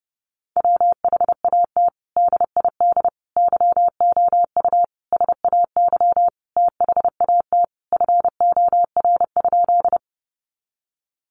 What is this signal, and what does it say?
Morse code, heard as a keyed tone, about 30 wpm, 700 Hz. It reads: WHAT DID YOU SAY THAT FOR?
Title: Sentences from Top 200 Words 30wpm